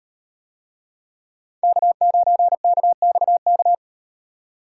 Morse code, heard as a keyed tone, about 38 wpm, 700 Hz. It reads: K9KXK